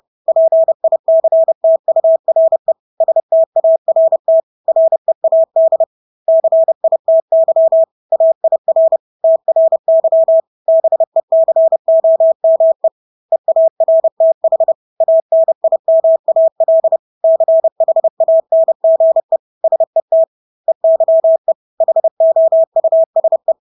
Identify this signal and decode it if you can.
Morse code, heard as a keyed tone, about 30 wpm, 650 Hz. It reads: PICTURE START READ CITY AIR TRY BECOME EARTH ANIMAL CHANGE SET EYE HOUSE
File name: Engelske ord 30-30 wpm